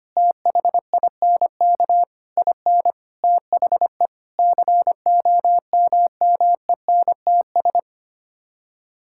Morse code, heard as a keyed tone, about 25 wpm, 700 Hz. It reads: THINK IN THE COMMENTS
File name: Sets of 9 Words - 25wpm